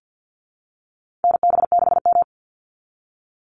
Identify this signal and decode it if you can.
Morse code, heard as a keyed tone, about 50 wpm, 700 Hz. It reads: NB6M
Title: Call Signs Repeated 2x - 50wpm